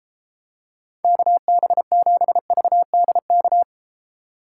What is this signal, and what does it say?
Morse code, heard as a keyed tone, about 33 wpm, 700 Hz. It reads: KB7VDK